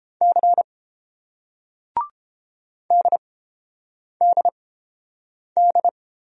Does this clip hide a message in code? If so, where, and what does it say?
Yes: Morse code from 0.2 to 5.9 s, heard as a keyed tone, about 30 wpm, spaced as for 8 wpm, 700 Hz. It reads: C DDD